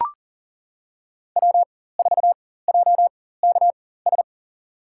Morse code, heard as a keyed tone, about 40 wpm, spaced as for 19 wpm, 700 Hz. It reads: W4JKS
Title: USA Call Signs - 5 Characters - 4x Character Spacing - 40wpm